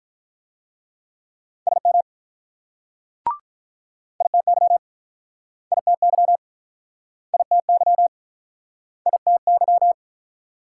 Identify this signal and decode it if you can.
Morse code, heard as a keyed tone, about 45 wpm, 700 Hz. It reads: SM ITY ITY ITY ITY